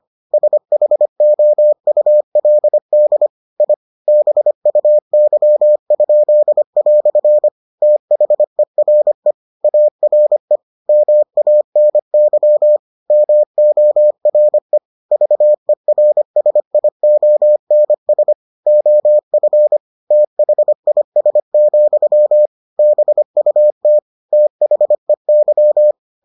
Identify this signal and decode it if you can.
Morse code, heard as a keyed tone, about 25 wpm, 600 Hz. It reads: SHOULD I BUY?" THERE ARE MANY MORE VERSIONS OF THIS, BUT THEY